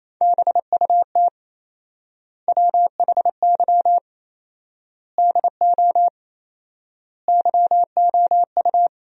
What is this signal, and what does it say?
Morse code, heard as a keyed tone, about 28 wpm, 700 Hz. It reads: BUT WHY DO YOU
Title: 4x Word-Spacing - Sentences from Top 200 Words - 28wpm